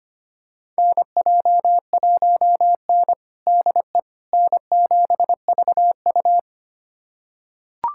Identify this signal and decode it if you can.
Morse code, heard as a keyed tone, about 25 wpm, 700 Hz. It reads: NJ1N DE N7VU
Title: USA Callsign DE Callsign - 4 Characters - 25wpm